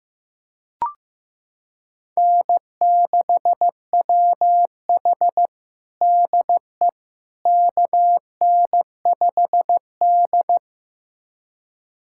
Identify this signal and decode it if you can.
Morse code, heard as a keyed tone, about 15 wpm, 700 Hz. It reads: N6WH DE KN5D
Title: USA Callsign DE Callsign - 4 Characters - 15wpm